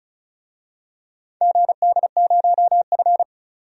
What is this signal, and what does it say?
Morse code, heard as a keyed tone, about 36 wpm, 700 Hz. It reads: GD0F
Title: DXCC-100 Call Signs - 4 Characters - 35wpm